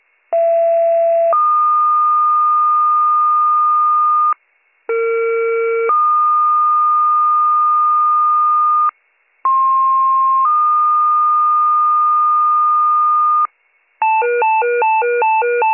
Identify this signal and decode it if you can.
Two-tone page, heard as a one-second tone, then a three-second tone, about 670 then 1220 Hz, about 460 then 1220 Hz, about 1060 then 1220 Hz. It is heard from 0.3 to 13.5 s.